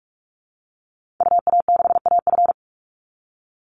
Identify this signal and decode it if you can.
Morse code, heard as a keyed tone, about 44 wpm, 700 Hz. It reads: UA6AF
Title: Call Signs Repeated 3x - 45wpm